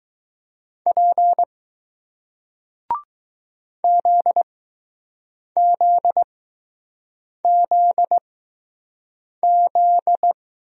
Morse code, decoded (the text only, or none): P ZZZZ